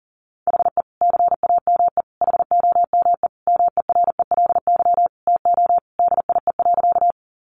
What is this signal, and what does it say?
Morse code, heard as a keyed tone, about 40 wpm, 700 Hz. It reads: HE CAME HOME MERELY TO DIE.
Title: Sentences from Top 1400 Words 40wpm